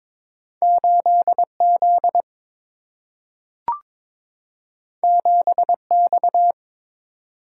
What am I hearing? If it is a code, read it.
Morse code, heard as a keyed tone, about 22 wpm, 700 Hz. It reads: 8 Z 7 X
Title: Single-Letter Single-Number 22wpm